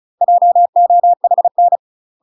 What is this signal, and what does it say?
Morse code, heard as a keyed tone, about 34 wpm, 700 Hz. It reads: JOHN
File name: UK MASTER QSOs TRAINING / 35WPM / UK MASTER QSOs 0024